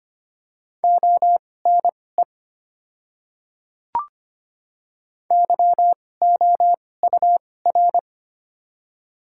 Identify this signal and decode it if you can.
Morse code, heard as a keyed tone, about 25 wpm, spaced as for 18 wpm, 690 Hz. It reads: ONE YOUR
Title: 2x Character Spacing - Top 100 Words - 25wpm